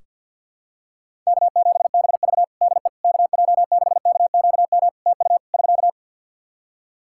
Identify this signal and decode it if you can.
Morse code, heard as a keyed tone, about 47 wpm, 700 Hz. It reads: K7BV DE KJ6KQM TU <SK>
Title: Practice QSO - Exchange 4 - Ending - 50wpm